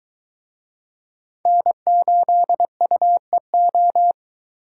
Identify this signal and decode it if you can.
Morse code, heard as a keyed tone, about 21 wpm, 700 Hz. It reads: N8UEO